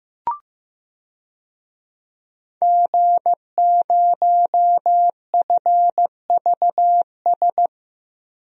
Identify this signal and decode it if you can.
Morse code, heard as a keyed tone, about 15 wpm, 700 Hz. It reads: G0FVS
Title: DXCC-100 Call Signs - 5 Characters - 15wpm